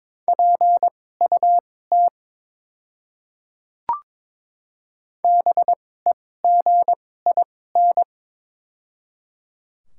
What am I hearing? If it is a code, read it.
Morse code, heard as a keyed tone, about 22 wpm, spaced as for 16 wpm, 690 Hz. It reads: PUT BEGIN
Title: 2x Character Spacing - Top 300 Words - 22wpm